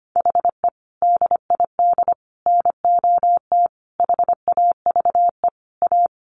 Morse code, heard as a keyed tone, about 25 wpm, 700 Hz. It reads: HE DID NOT HAVE A
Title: Sentences from Top 1300 Words 25wpm